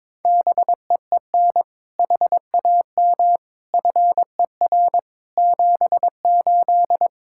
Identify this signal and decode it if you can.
Morse code, heard as a keyed tone, about 22 wpm, 700 Hz. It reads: BEEN HAM FER 78
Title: Practice QSO - Exchange 3 - More Info - 22wpm